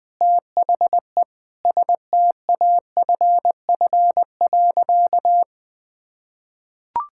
Morse code, heard as a keyed tone, about 19 wpm, 700 Hz. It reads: THE STAFF.
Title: Sentences with 5-Word Phrases - 20wpm